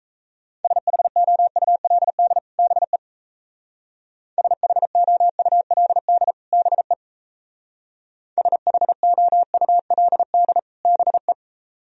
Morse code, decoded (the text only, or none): SHOULD BE SHOULD BE SHOULD BE